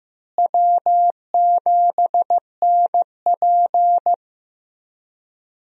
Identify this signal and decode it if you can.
Morse code, heard as a keyed tone, about 15 wpm, 700 Hz. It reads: W7NP
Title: USA Call Signs - 4 Characters - Repeated 3x - 15wpm